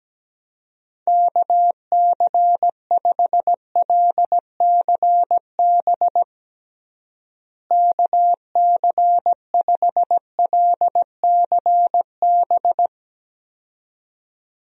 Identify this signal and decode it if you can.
Morse code, heard as a keyed tone, about 17 wpm, 700 Hz. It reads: KC5LCB KC5LCB
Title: USA Call Signs - 6 Characters - Repeated 2x - 17wpm